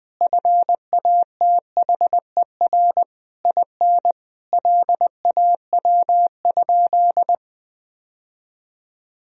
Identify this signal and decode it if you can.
Morse code, heard as a keyed tone, about 20 wpm, 700 Hz. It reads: FATHER IN LAW?
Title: Sentences from Top 1000 Words - Review Part 4 - 20wpm